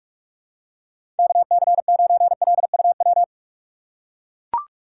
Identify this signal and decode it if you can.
Morse code, heard as a keyed tone, about 42 wpm, 700 Hz. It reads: KC9LUW
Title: USA Call Signs - 6 Characters - 4x Character Spacing - 45wpm